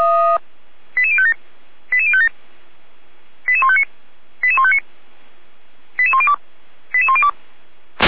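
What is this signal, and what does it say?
Five-tone selective call (ZVEI), heard as quick runs of five tones, tones of about 70 ms. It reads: 88057 88057 88159 88159 88102 88102
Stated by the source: Five-tone ZVEI selective call received by SDR, from audio.f85360.feuerwehr-orlamuende.org (Feuerwehr Orlamünde fire brigade).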